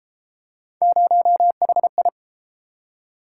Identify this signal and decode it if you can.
Morse code, heard as a keyed tone, about 31 wpm, 700 Hz. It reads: 0HI